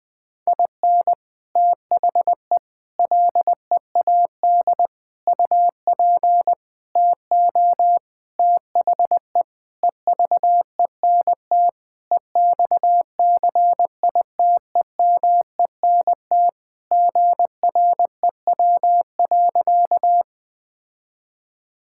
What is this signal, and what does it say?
Morse code, heard as a keyed tone, about 21 wpm, 700 Hz. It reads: IN THE LEAD UP TO THE EVENT EXCITEMENT GREW.
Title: Sentences with 6-Word Phrases - 20wpm